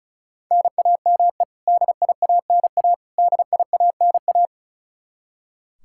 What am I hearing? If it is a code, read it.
Morse code, heard as a keyed tone, about 36 wpm, 700 Hz. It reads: NAME DIANA DIANA
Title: Practice QSO - Exchange 1 - Beginning - 35wpm